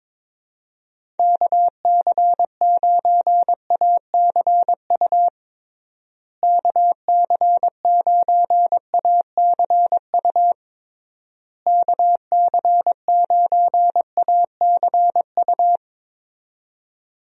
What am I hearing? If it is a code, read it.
Morse code, heard as a keyed tone, about 21 wpm, 700 Hz. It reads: KC9ACU KC9ACU KC9ACU
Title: USA Call Signs - 6 Characters - Repeated 3x - 22wpm